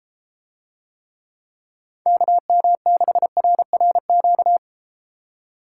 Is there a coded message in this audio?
Yes: Morse code from 2.1 to 4.6 s, heard as a keyed tone, about 32 wpm, 700 Hz. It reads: KM6RRQ